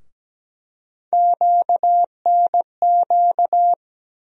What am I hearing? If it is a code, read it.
Morse code, heard as a keyed tone, about 17 wpm, 700 Hz. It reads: QNQ
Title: USA Call Sign Postfixes - 3 Letters 17wpm